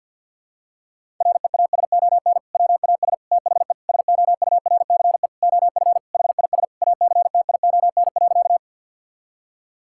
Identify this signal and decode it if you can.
Morse code, heard as a keyed tone, about 50 wpm, 700 Hz. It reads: REASON WAS THE SOURCE OF HIS ACTION.
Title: Sentences from Top 1000 Words - Review Part 5 - 50wpm